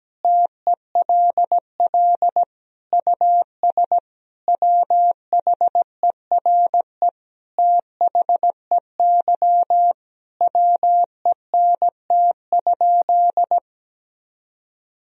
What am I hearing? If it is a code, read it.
Morse code, heard as a keyed tone, about 17 wpm, 700 Hz. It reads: TELL US WHERE THEY WENT?
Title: Sentences from Top 1000 Words - Review Part 3 - 17wpm